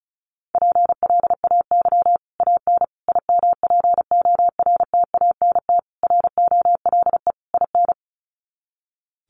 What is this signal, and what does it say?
Morse code, heard as a keyed tone, about 34 wpm, 700 Hz. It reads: PLAY AN IMPORTANT ROLE IN
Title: Sets of 5 Words Encore 35wpm